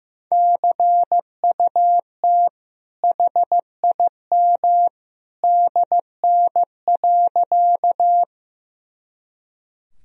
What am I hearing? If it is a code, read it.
Morse code, heard as a keyed tone, about 15 wpm, 700 Hz. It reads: CUT HIM DN.
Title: Sentences from Top 300 Words and Abbreviations 15wpm